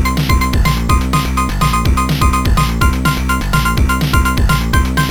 There is no code.